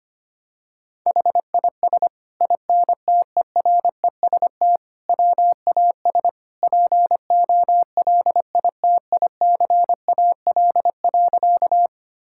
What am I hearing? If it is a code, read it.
Morse code, heard as a keyed tone, about 25 wpm, 700 Hz. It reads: HIS INTEREST WAS POLITICAL.
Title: Sentences from Top 1000 Words - Review Part 5 - 25wpm